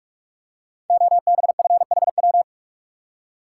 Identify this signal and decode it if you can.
Morse code, heard as a keyed tone, about 44 wpm, 700 Hz. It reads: OBFHW
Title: Code Groups - 5 Letters - 45wpm